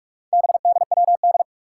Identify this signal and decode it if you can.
Morse code, heard as a keyed tone, about 46 wpm, 700 Hz. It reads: BDWD
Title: Four Challenging Letter-Number Combinations 45wpm